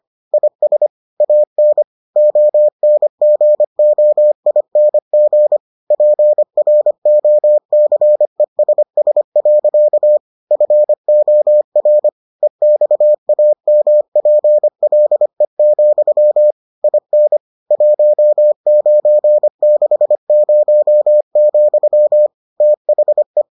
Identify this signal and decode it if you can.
Morse code, heard as a keyed tone, about 25 wpm, 600 Hz. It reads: IS AN ONGOING PROCESS. FOR EXAMPLE, IN 1960, THE